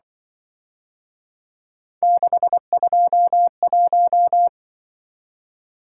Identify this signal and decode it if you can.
Morse code, heard as a keyed tone, about 23 wpm, 700 Hz. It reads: 621
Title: Speed-Racing - Three Letter-Number Combinations - 20wpm